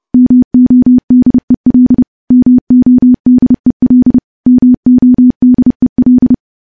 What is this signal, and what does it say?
Morse code, heard as a keyed tone, about 30 wpm, 250 Hz. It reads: MODEL MODEL MODEL
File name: morse / morse-training-30WPM